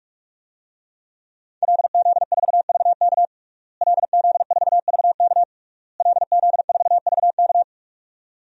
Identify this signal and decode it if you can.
Morse code, heard as a keyed tone, about 46 wpm, 700 Hz. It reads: LZ4VK LZ4VK LZ4VK
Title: DXCC-100 Call Signs - 5 Characters - Repeated 3x - 45wpm